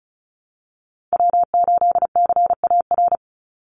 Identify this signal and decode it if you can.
Morse code, heard as a keyed tone, about 34 wpm, 700 Hz. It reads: W8CAR
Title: Call Signs Repeated 3x - 35wpm